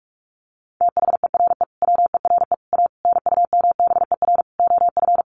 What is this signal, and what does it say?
Morse code, heard as a keyed tone, about 44 wpm, 700 Hz. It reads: THERE WERE A NUMBER OF
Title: Sets of 5 Words Encore2 45wpm